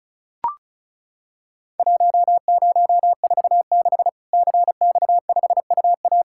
Morse code, heard as a keyed tone, about 34 wpm, 700 Hz. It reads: 1046 CX5UA
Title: Worked All Europe DX Contest - QTC Exchange - 35wpm